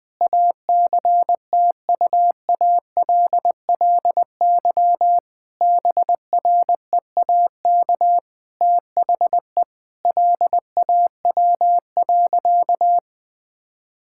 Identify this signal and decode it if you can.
Morse code, heard as a keyed tone, about 20 wpm, 700 Hz. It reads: ACTUALLY BREAK THE LAW.
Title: Sentences from Top 1000 Words - Review Part 6 - 20wpm